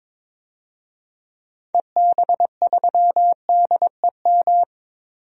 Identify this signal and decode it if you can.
Morse code, heard as a keyed tone, about 22 wpm, 700 Hz. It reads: EB3DEM
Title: DXCC-100 Call Signs - 6 Characters - 22wpm